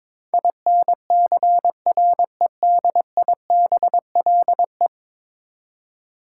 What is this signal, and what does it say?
Morse code, heard as a keyed tone, about 22 wpm, 700 Hz. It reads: INCREDIBLE
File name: practice / 22wpm / 10 Letter Words - First 3 Letters Given - 22wpm